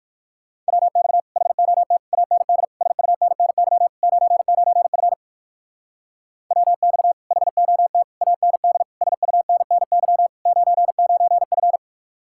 Morse code, heard as a keyed tone, about 54 wpm, 700 Hz. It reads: WX HOT AND SUNNY 99F WX HOT AND SUNNY 99F
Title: Speed-Racing Common QSO Phrases - 45wpm